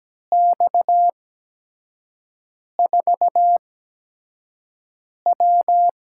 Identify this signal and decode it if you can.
Morse code, heard as a keyed tone, about 17 wpm, spaced as for 5 wpm, 700 Hz. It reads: X4W
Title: USA Call Signs - 8x Character Spacing - 17wpm